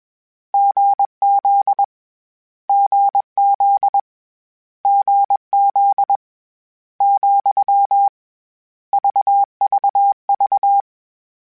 Morse code, heard as a keyed tone, about 21 wpm, 800 Hz. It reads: GZ GZ GZ , VVV
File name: Difficult Combos / Difficult Combos-mp3s / GZ GZ7 @20